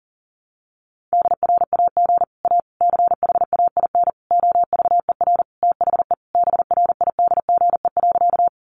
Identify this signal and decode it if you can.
Morse code, heard as a keyed tone, about 40 wpm, 700 Hz. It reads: DRAG A CHAIN OVER THE BRIDGE.
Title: Sentences from Top 1500 Words 40wpm